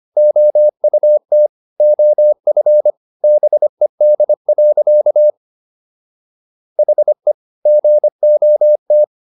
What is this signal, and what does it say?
Morse code, heard as a keyed tone, about 25 wpm, 600 Hz. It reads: OUT OF BED. HE GOT